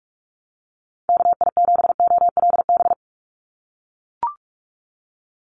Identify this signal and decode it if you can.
Morse code, heard as a keyed tone, about 44 wpm, 700 Hz. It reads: KI7OLB
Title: USA Call Signs 45wpm